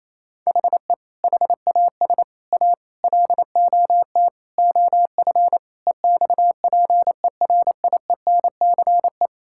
Morse code, heard as a keyed tone, about 28 wpm, 700 Hz. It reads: HE HAS A LOT OF EXPERIENCE
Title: Sentences from Top 1000 Words - Review Part 6 - 28wpm